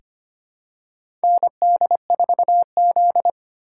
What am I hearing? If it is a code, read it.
Morse code, heard as a keyed tone, about 25 wpm, 700 Hz. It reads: ND4Z
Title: USA Call Signs - 4 Characters - 2x Character Spacing - 25wpm